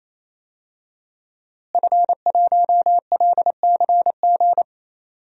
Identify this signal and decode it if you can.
Morse code, heard as a keyed tone, about 27 wpm, 700 Hz. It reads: F1LCG
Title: DXCC-100 Call Signs - 5 Characters - Repeated 3x - 28wpm